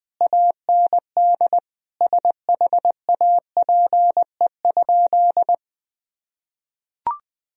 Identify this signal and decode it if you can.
Morse code, heard as a keyed tone, about 20 wpm, 700 Hz. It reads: AND SHAPE?
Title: Sentences from Top 1000 Words - Review Part 3 - 20wpm